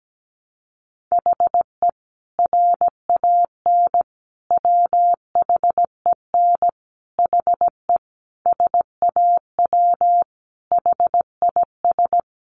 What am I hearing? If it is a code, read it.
Morse code, heard as a keyed tone, about 17 wpm, 700 Hz. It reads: HE RAN WHEN HE SAW HIS
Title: Sentences from Top 500 Words 17wpm